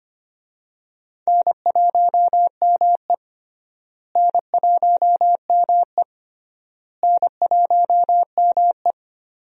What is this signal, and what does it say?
Morse code, heard as a keyed tone, about 25 wpm, 700 Hz. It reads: N1ME N1ME N1ME
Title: USA Call Signs - 4 Characters - Repeated 3x - 25wpm